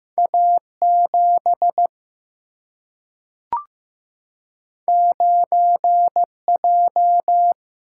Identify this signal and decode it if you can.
Morse code, heard as a keyed tone, about 15 wpm, 700 Hz. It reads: A7 9J
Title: Single-Letter Single-Number 15wpm